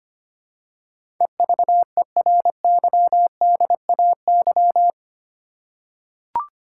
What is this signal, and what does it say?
Morse code, heard as a keyed tone, about 25 wpm, 700 Hz. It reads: EVERYDAY
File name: Eight Letter Words - 25wpm